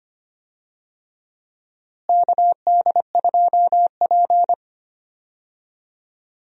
Morse code, heard as a keyed tone, about 25 wpm, 700 Hz. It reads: KD2P
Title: USA Call Signs - 4 Characters - Repeated 3x - 25wpm